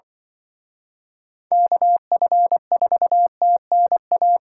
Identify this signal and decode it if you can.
Morse code, heard as a keyed tone, about 24 wpm, 700 Hz. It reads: KF4TNA
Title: Speed Racing - USA 2x3 Call Signs 20wpm